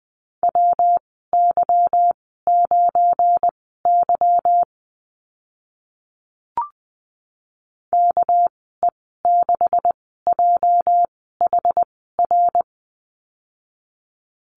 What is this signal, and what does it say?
Morse code, heard as a keyed tone, about 20 wpm, spaced as for 14 wpm, 700 Hz. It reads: WY9Y KE6JHR
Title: 219 Lesson 35g - Number 0 - Call Signs 20wpm